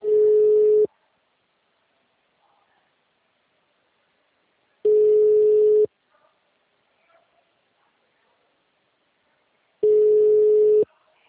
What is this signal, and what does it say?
Ringback tone (Europe, 425 Hz).